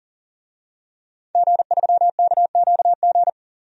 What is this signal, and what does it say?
Morse code, heard as a keyed tone, about 40 wpm, 700 Hz. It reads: G3KQG
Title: DXCC-100 Call Signs - 5 Characters - 40wpm